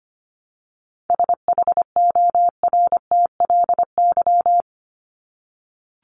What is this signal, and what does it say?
Morse code, heard as a keyed tone, about 25 wpm, 700 Hz. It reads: SHORTLY
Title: LY Suffix - Top 100 Words with Base Words 25wpm